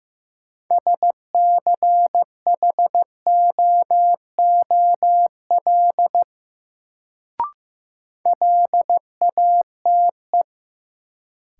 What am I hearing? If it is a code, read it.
Morse code, heard as a keyed tone, about 15 wpm, 700 Hz. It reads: SCHOOL LATE
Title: Sentences from Top 300 Words - New Word Drill - 15wpm